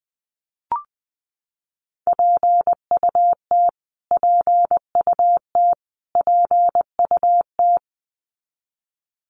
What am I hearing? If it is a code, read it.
Morse code, heard as a keyed tone, about 20 wpm, 700 Hz. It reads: PUT PUT PUT
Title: Top 500 Words Repeated 20wpm